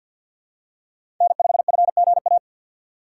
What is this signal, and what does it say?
Morse code, heard as a keyed tone, about 52 wpm, 700 Hz. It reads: N5FGA
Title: USA Call Signs - 5 Characters - 2x Character Spacing - 50wpm